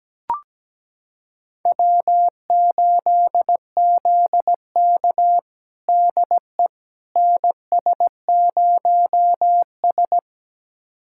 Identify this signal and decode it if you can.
Morse code, heard as a keyed tone, about 17 wpm, 700 Hz. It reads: W8ZK DE NS0S
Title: USA Callsign DE Callsign - 4 Characters - 17wpm